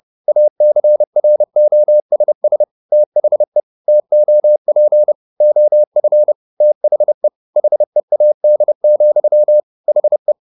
Morse code, heard as a keyed tone, about 30 wpm, 600 Hz. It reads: ACROSS THE TOP OF THE HEAD, HE